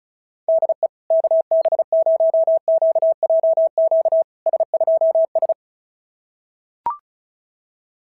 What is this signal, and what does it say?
Morse code, heard as a keyed tone, about 36 wpm, 660 Hz. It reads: DE KB0QJQ S2S
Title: SOTA QSO Practice 35wpm